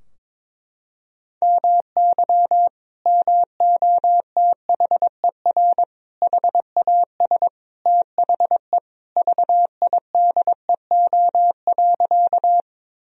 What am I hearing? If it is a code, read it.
Morse code, heard as a keyed tone, about 22 wpm, 700 Hz. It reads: MY MOTHER HAS THE VIDEO.
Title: Sentences from Top 1000 Words - Review Part 4 - 22wpm